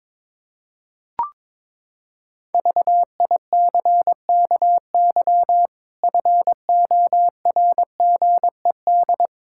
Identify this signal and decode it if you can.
Morse code, heard as a keyed tone, about 22 wpm, 700 Hz. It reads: VICKY FORGED
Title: Pangram Challenge v2 - 22wpm